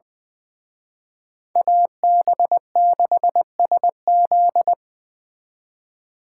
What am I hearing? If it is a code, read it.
Morse code, heard as a keyed tone, about 25 wpm, 700 Hz. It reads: AB6SZ
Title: Speed Racing - USA 2x2 Call Signs 20wpm